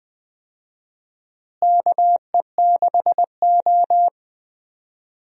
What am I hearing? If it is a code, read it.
Morse code, heard as a keyed tone, about 20 wpm, 700 Hz. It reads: KE6O
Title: USA Call Signs - 4 Characters - Repeated 3x - 20wpm